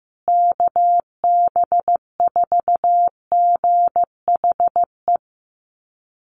Morse code, heard as a keyed tone, about 15 wpm, 700 Hz. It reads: KB4GHE
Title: USA Call Signs Repeated 3x 15wpm